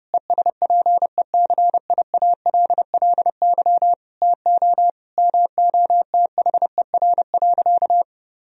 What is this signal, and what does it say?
Morse code, heard as a keyed tone, about 30 wpm, 700 Hz. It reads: ESPECIALLY TO MOTHER.
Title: Sentences from Top 1000 Words - Review Part 4 - 30wpm